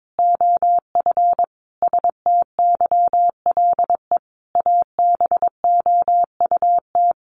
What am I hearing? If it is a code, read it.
Morse code, heard as a keyed tone, about 22 wpm, 700 Hz. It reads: OF STYLE ABOUT
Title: Sentences from Top 800 Words 22wpm